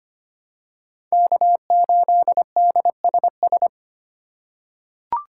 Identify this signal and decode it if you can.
Morse code, heard as a keyed tone, about 25 wpm, 700 Hz. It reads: K8DSS
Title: K1USN SST QSO Practice - 25wpm